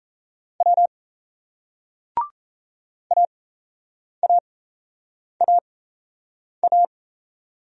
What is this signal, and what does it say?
Morse code, heard as a keyed tone, about 38 wpm, spaced as for 8 wpm, 700 Hz. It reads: W AAAA